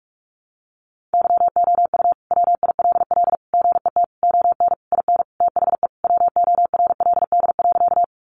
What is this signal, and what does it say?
Morse code, heard as a keyed tone, about 44 wpm, 700 Hz. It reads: YOU WILL GET ON IN THE WORLD.
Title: Sentences from Top 200 Words 45wpm